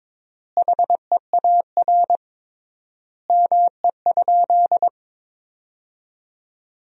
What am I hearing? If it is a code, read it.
Morse code, heard as a keyed tone, about 22 wpm, 700 Hz. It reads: HEAR ME?
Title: 3x Word-Spacing - Sentences from Top 300 Words - 22wpm